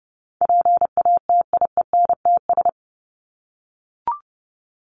Morse code, heard as a keyed tone, about 30 wpm, 700 Hz. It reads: PATIENTS
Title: ES:S Suffix - Top 100 Words with Base Words 30wpm